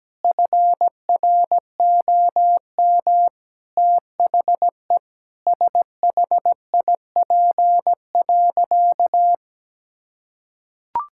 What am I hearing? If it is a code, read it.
Morse code, heard as a keyed tone, about 17 wpm, 700 Hz. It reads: FROM THE SHIP.
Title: Sentences from Top 1000 Words - Review Part 5 - 17wpm